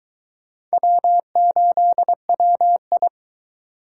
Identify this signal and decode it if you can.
Morse code, heard as a keyed tone, about 22 wpm, 700 Hz. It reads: W8WI